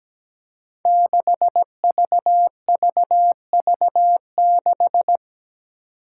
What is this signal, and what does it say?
Morse code, heard as a keyed tone, about 17 wpm, 700 Hz. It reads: 6VVV6
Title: Five Challenging Letter-Number Combinations 17wpm